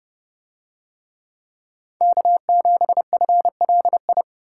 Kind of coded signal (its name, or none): Morse code